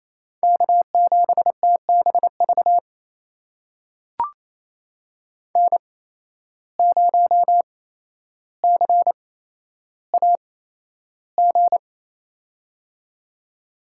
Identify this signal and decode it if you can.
Morse code, heard as a keyed tone, about 28 wpm, spaced as for 12 wpm, 700 Hz. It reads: K7TBV N0CAG
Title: USA Call Signs - 5 Characters - 8x Character Spacing - 28wpm